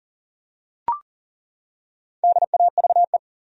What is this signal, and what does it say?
Morse code, heard as a keyed tone, about 40 wpm, 700 Hz. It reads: DAVE
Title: USA Common Names - Max 5 Letters - 40wpm